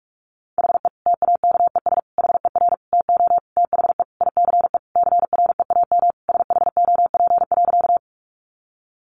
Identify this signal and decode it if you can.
Morse code, heard as a keyed tone, about 44 wpm, 700 Hz. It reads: HE TAKES HER TO THE ICE CREAM SHOP.
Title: Easy Phrases 45wpm